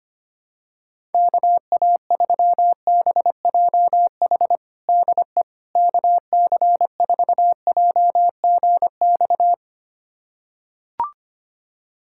Morse code, heard as a keyed tone, about 25 wpm, 700 Hz. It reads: KA3BJH DE KC4JGX
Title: USA Callsign DE Callsign - 6 Characters - 25wpm